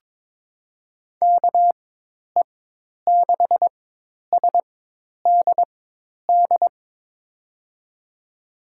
Morse code, heard as a keyed tone, about 22 wpm, spaced as for 10 wpm, 700 Hz. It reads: KE6SDD
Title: USA Call Signs - 6 Characters - 4x Character Spacing - 22wpm